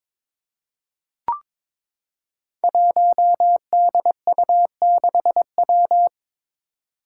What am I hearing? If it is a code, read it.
Morse code, heard as a keyed tone, about 22 wpm, 700 Hz. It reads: 1DU6W
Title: Five Challenging Letter-Number Combinations 22wpm